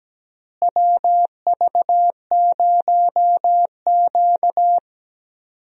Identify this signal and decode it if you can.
Morse code, heard as a keyed tone, about 17 wpm, 700 Hz. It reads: WV0Q
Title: USA Call Signs - 4 Characters - Repeated 3x - 17wpm